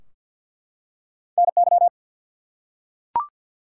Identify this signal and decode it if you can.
Morse code, heard as a keyed tone, about 50 wpm, 700 Hz. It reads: NY
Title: US States and Canadian Provinces 50wpm